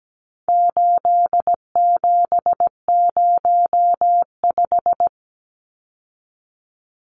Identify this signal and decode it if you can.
Morse code, heard as a keyed tone, about 17 wpm, 700 Hz. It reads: 8705
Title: Numbers - 4-Digit 17wpm